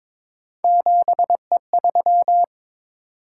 Morse code, heard as a keyed tone, about 22 wpm, 700 Hz. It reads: 7E3
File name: practice / 22wpm / Three Letter-Number Combinations - 22wpm